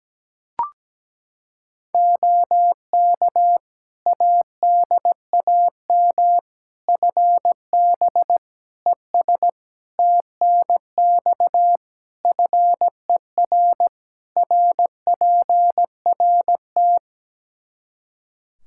Morse code, heard as a keyed tone, about 17 wpm, 700 Hz. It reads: OK ADAM FB ES TNX FER RPRT
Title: Practice QSO - Exchange 2 - Info - 17wpm